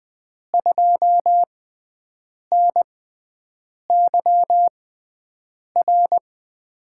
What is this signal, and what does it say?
Morse code, heard as a keyed tone, about 20 wpm, spaced as for 7 wpm, 700 Hz. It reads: 2NYR
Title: USA Call Signs - 6 Characters - 6x Character Spacing - 20wpm